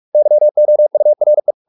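Morse code, heard as a keyed tone, about 44 wpm, 600 Hz. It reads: YOURE
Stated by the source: Morse practice at 45 wpm, the novel Never by Ken Follett in English, from CW med Gnister.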